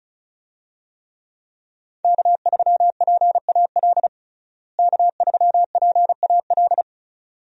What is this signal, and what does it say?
Morse code, heard as a keyed tone, about 34 wpm, 700 Hz. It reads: K3PAL K3PAL
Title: USA Call Signs - 5 Characters - Repeated 2x - 35wpm